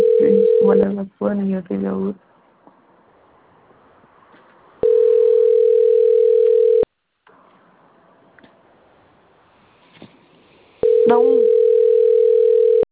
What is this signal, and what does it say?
Ringback tone (North America), 440 + 480 Hz, with speech.
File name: rg-1600-unknown- - - .40678